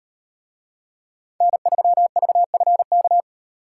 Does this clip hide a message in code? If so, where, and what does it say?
Yes: Morse code from 1.4 to 3.2 s, heard as a keyed tone, about 38 wpm, 700 Hz. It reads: N3VFK